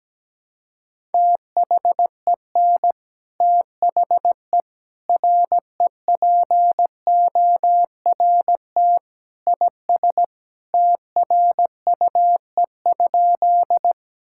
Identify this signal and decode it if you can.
Morse code, heard as a keyed tone, about 17 wpm, 700 Hz. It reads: THEN THE REPORT IS TRUE?
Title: Sentences from Top 1000 Words - Review Part 6 - 17wpm